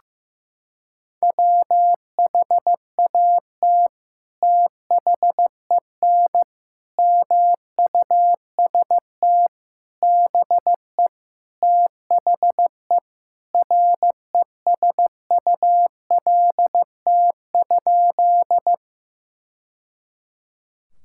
Morse code, heard as a keyed tone, about 15 wpm, 700 Hz. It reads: WHAT THEN MUST BE THE RESULT?
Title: Sentences from Top 1000 Words - Review Part 5 - 15wpm